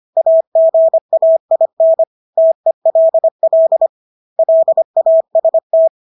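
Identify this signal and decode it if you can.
Morse code, heard as a keyed tone, about 25 wpm, 650 Hz. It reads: AGAIN TELL LAST
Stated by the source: Morse practice of English words at 25 wpm from CW med Gnister.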